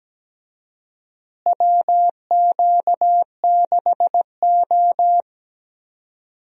Morse code, heard as a keyed tone, about 17 wpm, 700 Hz. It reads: WQ6O
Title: USA Call Signs - 4 Characters - Repeated 3x - 17wpm